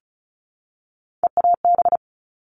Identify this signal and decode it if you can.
Morse code, heard as a keyed tone, about 34 wpm, 700 Hz. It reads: EAB
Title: All Cut Numbers - 3-Digit 35wpm